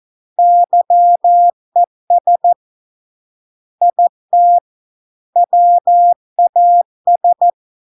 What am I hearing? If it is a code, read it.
Morse code, heard as a keyed tone, about 14 wpm, 700 Hz. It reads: YES IT WAS